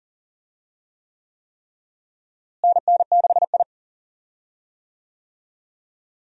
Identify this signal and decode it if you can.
Morse code, heard as a keyed tone, about 39 wpm, 700 Hz. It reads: NN6I
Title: USA Call Signs - 4 Characters - Repeated 3x - 40wpm